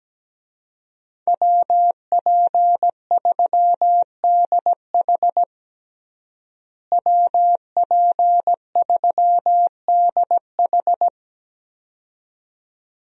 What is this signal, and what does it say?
Morse code, heard as a keyed tone, about 17 wpm, 700 Hz. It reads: WP3DH WP3DH
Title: USA Call Signs - 5 Characters - Repeated 2x - 17wpm